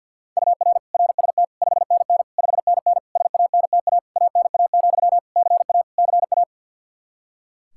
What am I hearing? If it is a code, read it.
Morse code, heard as a keyed tone, about 46 wpm, 700 Hz. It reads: UR RST 5NN 5NN SANTA ANA, CA CA
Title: Practice QSO - Exchange 1 - Beginning - 50wpm